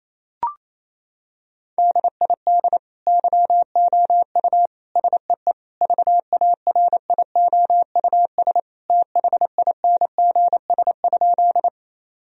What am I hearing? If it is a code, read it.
Morse code, heard as a keyed tone, about 28 wpm, 700 Hz. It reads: DID YOU SEE VARIOUS THINGS?
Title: Sentences from Top 1000 Words - Review Part 4 - 28wpm